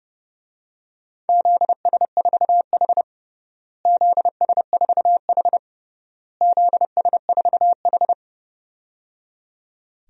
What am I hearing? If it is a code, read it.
Morse code, heard as a keyed tone, about 30 wpm, 700 Hz. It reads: ZS4H ZS4H ZS4H
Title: DXCC-100 Call Signs - 4 Characters - Repeated 3x - 30wpm